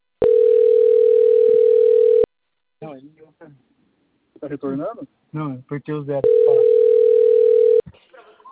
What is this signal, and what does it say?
Ringback tone (North America), 440 + 480 Hz, with speech.
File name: rg-1600-unknown- - - .34724